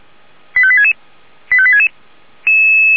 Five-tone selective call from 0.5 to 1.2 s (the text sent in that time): 85866